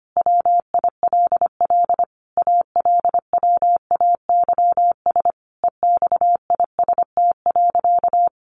Morse code, heard as a keyed tone, about 25 wpm, 700 Hz. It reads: WILL ALWAYS EXIST.
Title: Sentences from Top 1500 Words 25wpm